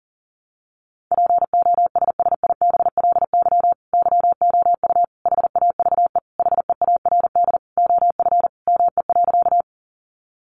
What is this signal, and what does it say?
Morse code, heard as a keyed tone, about 40 wpm, 700 Hz. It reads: POSSIBLY YOU HAVE HEARD OF ME.
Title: Sentences from Top 1300 Words 40wpm